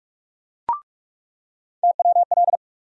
Morse code, heard as a keyed tone, about 44 wpm, 700 Hz. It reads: TWL
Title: USA Call Sign Postfixes - 3 Letters 45wpm